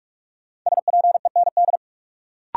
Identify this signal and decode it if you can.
Morse code, heard as a keyed tone, about 46 wpm, 700 Hz. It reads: SPEND
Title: Sets of 5 Words - New Word Drill - 45wpm